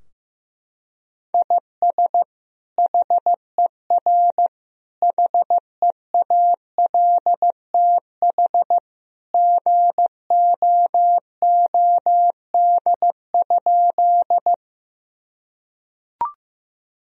Morse code, heard as a keyed tone, about 15 wpm, 700 Hz. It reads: IS HER HEALTH GOOD?
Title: Sentences from Top 1000 Words - Review Part 3 - 15wpm